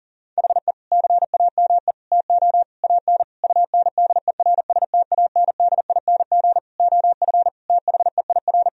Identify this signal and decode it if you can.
Morse code, heard as a keyed tone, about 39 wpm, 700 Hz. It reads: HE CAME TO AN UNDERSTANDING OF THEIR
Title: Sentences from Top 1000 Words - Review Part 3 - 40wpm